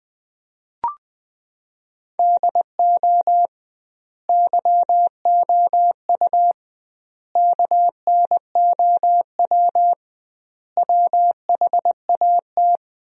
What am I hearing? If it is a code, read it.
Morse code, heard as a keyed tone, about 20 wpm, 700 Hz. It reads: DO YOU KNOW WHAT
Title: 2x Word-Spacing - Sentences from Top 100 Words - 20wpm